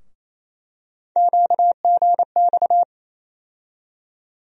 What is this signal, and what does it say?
Morse code, heard as a keyed tone, about 28 wpm, 700 Hz. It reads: QGX
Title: USA Call Sign Postfixes - 3 Letters 28wpm